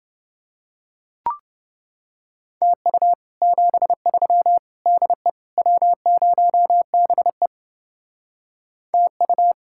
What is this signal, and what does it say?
Morse code, heard as a keyed tone, about 30 wpm, 700 Hz. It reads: TU 73 DE W0BE TU